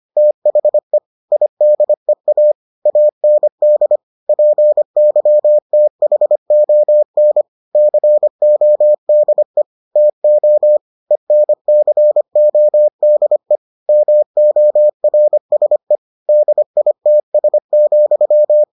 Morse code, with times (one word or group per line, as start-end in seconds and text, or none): 0.2-1.0 THE
1.3-2.5 IDEA
2.9-4.0 AND
4.3-7.4 PYTHON
7.8-9.6 CODE
10.0-10.8 TO
11.1-13.6 ENCODE
13.9-16.0 MORSE
16.3-18.6 DITS,